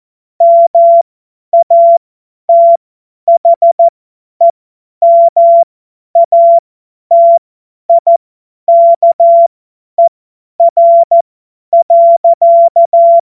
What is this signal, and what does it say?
Morse code, heard as a keyed tone, about 14 wpm, spaced as for 10 wpm, 680 Hz. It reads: MATHEMATIKER.
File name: Morsekurs / Witze / Witz 21. Juni 2022 50 F70